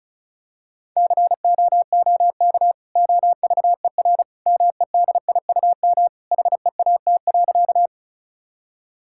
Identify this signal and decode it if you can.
Morse code, heard as a keyed tone, about 36 wpm, 700 Hz. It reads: COOK OVER MEDIUM HEAT.
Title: Sentences from Top 1000 Words - Review Part 6 - 35wpm